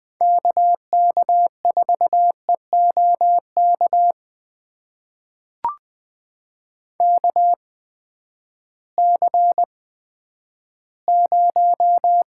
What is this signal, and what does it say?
Morse code, heard as a keyed tone, about 20 wpm, spaced as for 11 wpm, 700 Hz. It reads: KK4EOK KC0
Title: USA Call Signs - 6 Characters - 8x Character Spacing - 20wpm